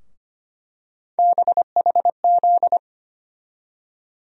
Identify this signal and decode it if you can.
Morse code, heard as a keyed tone, about 25 wpm, 700 Hz. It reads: BHZ
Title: USA Call Sign Postfixes - 3 Letters 25wpm